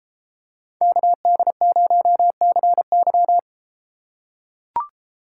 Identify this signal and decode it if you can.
Morse code, heard as a keyed tone, about 31 wpm, 700 Hz. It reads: KD0CY